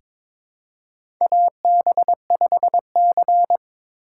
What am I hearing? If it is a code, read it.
Morse code, heard as a keyed tone, about 22 wpm, 700 Hz. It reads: AB5C
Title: K1USN SST QSO Practice - 22wpm